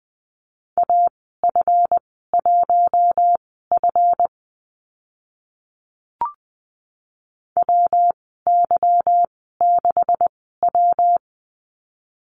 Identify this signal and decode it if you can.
Morse code, heard as a keyed tone, about 20 wpm, spaced as for 15 wpm, 700 Hz. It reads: AF1F WY6W
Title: 154 Lesson 25g - Letter p - Call Signs 20wpm